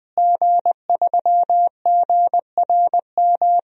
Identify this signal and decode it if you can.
Morse code, heard as a keyed tone, about 20 wpm, 700 Hz. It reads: G3GRM
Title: DXCC-100 Call Signs - 5 Characters - Repeated 3x - 20wpm